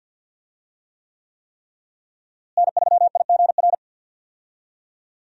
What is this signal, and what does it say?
Morse code, heard as a keyed tone, about 52 wpm, 700 Hz. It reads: N3IGR
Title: USA Call Signs - 5 Characters - Repeated 3x - 50wpm